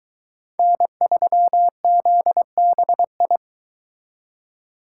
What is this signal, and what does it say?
Morse code, heard as a keyed tone, about 23 wpm, 700 Hz. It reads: N3ZBI